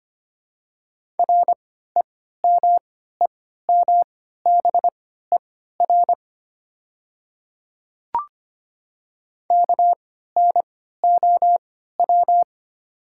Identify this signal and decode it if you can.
Morse code, heard as a keyed tone, about 25 wpm, spaced as for 14 wpm, 700 Hz. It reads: REMEMBER KNOW
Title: 3x Character Spacing - Top 500 Words - 25wpm